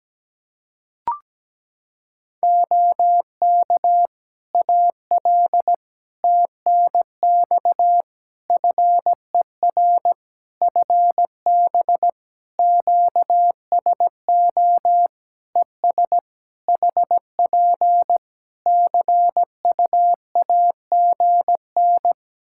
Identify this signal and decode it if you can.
Morse code, heard as a keyed tone, about 17 wpm, 700 Hz. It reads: OK AL TNX FER FB QSO ES HP CUAGN
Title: Practice QSO - Exchange 4 - Ending - 17wpm